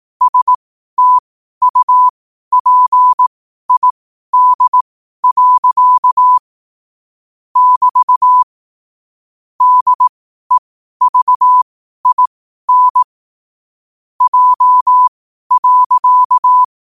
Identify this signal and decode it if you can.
Morse code, heard as a keyed tone, about 18 wpm, spaced as for 12 wpm, 1000 Hz. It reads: STUPID. <BT> DEVIN J.
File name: qotdfiles / 12wpm